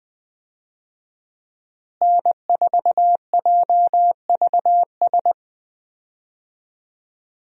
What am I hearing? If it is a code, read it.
Morse code, heard as a keyed tone, about 20 wpm, 700 Hz. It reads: N4JVS
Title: USA Call Signs - 5 Characters - Repeated 3x - 20wpm